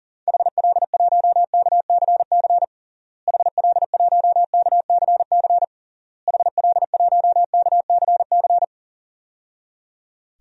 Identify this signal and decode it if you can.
Morse code, heard as a keyed tone, about 40 wpm, 700 Hz. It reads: HL1KCC HL1KCC HL1KCC
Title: DXCC-100 Call Signs - 6 Characters - Repeated 3x - 40wpm